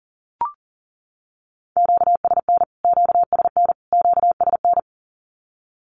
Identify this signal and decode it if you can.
Morse code, heard as a keyed tone, about 40 wpm, 700 Hz. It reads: QSN QSN QSN
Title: Q-Codes Repeated 3x 40wpm